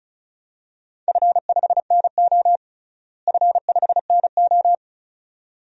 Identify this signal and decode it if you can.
Morse code, heard as a keyed tone, about 36 wpm, 700 Hz. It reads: F5NO F5NO
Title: DXCC-100 Call Signs - 4 Characters - Repeated 2x - 35wpm